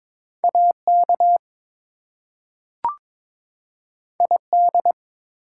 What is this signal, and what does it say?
Morse code, heard as a keyed tone, about 22 wpm, 700 Hz. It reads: AK ID
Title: US States and Canadian Provinces 22wpm